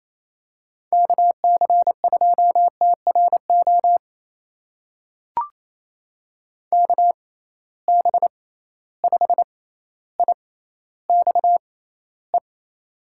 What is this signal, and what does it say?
Morse code, heard as a keyed tone, about 28 wpm, spaced as for 14 wpm, 700 Hz. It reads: KC2TRO KB5IXE
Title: USA Call Signs - 6 Characters - 6x Character Spacing - 28wpm